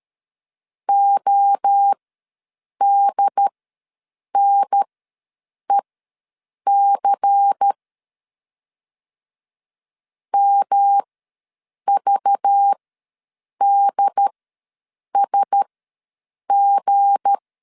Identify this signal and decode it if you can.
Morse code, heard as a keyed tone, about 13 wpm, spaced as for 7 wpm, 800 Hz. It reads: ODNEC MVDSG
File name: LEZIONI / 7 GF9Z / 7 G ws 12wpm 3140